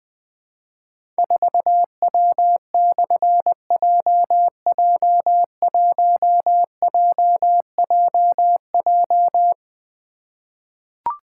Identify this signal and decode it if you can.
Morse code, heard as a keyed tone, about 20 wpm, 700 Hz. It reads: 4W/JJ1JJJ
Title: Difficult Call Signs 20wpm